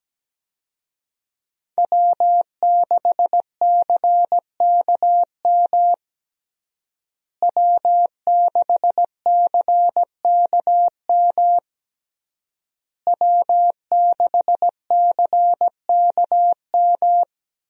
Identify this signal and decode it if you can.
Morse code, heard as a keyed tone, about 17 wpm, 700 Hz. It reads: W6CKM W6CKM W6CKM
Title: USA Call Signs - 5 Characters - Repeated 3x - 17wpm